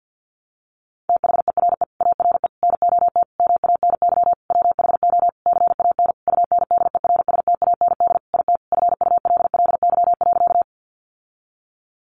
Morse code, heard as a keyed tone, about 50 wpm, 700 Hz. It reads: THERE ARE NOT MANY WHO CAN UNDERSTAND IT FULLY.
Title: Sentences from Top 1300 Words 50wpm